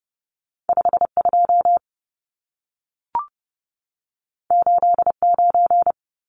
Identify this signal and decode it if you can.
Morse code, heard as a keyed tone, about 30 wpm, 700 Hz. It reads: 52 89
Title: Numbers - 2-Digit 30wpm